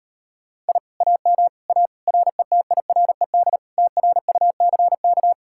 Morse code, heard as a keyed tone, about 38 wpm, 700 Hz. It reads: I AM A RETIRED TRUCK